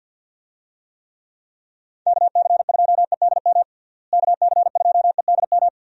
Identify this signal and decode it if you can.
Morse code, heard as a keyed tone, about 51 wpm, 700 Hz. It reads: KC2EDM KC2EDM
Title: USA Call Signs - 6 Characters - Repeated 2x - 50wpm